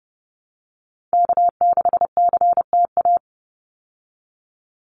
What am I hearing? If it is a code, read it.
Morse code, heard as a keyed tone, about 30 wpm, 700 Hz. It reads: K6CTA
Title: Call Signs Repeated 2x - 30wpm